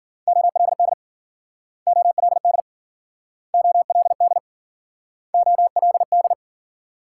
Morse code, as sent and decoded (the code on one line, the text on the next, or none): --- .-.. -.. / --- .-.. -.. / --- .-.. -.. / --- .-.. -..
OLD OLD OLD OLD